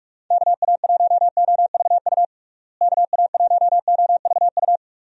Morse code, heard as a keyed tone, about 41 wpm, 700 Hz. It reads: KA1OVU KA1OVU
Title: USA Call Signs - 6 Characters - Repeated 3x - 45wpm